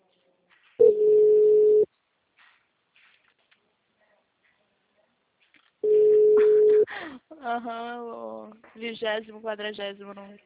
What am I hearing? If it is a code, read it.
Ringback tone (Europe, 425 Hz), with speech.
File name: out- -2002- - - .9960